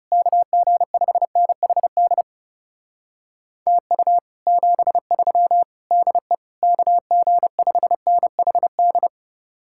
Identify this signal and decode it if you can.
Morse code, heard as a keyed tone, about 38 wpm, 700 Hz. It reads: KG5NHD TU 73 DE KG5NHD
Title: Speed-Racing Common QSO Phrases - 30wpm